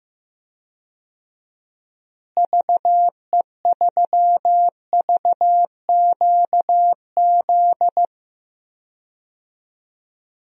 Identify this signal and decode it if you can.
Morse code, heard as a keyed tone, about 15 wpm, 700 Hz. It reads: VE3VQZ
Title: DXCC-100 Call Signs - 6 Characters - Repeated 2x - 15wpm